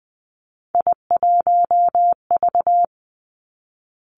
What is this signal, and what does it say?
Morse code, heard as a keyed tone, about 20 wpm, 700 Hz. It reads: I1V
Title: Three Challenging Letter-Number Combinations 20wpm